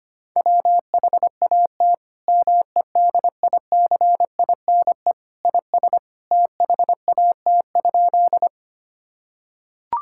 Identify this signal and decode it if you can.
Morse code, heard as a keyed tone, about 25 wpm, 700 Hz. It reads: WHAT MEDICINE IS THAT?
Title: Sentences from Top 1000 Words - Review Part 6 - 25wpm